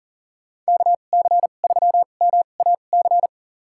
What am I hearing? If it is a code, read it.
Morse code, heard as a keyed tone, about 40 wpm, spaced as for 29 wpm, 700 Hz. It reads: KC3MAC
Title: USA Call Signs - 6 Characters - 2x Character Spacing - 40wpm